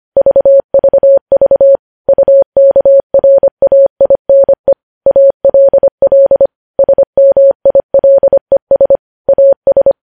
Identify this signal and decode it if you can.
Morse code, heard as a keyed tone, about 25 wpm, 550 Hz. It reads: VVV UKRAINE ALL SMILES AS
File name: podcast / News headlines-1x-25x25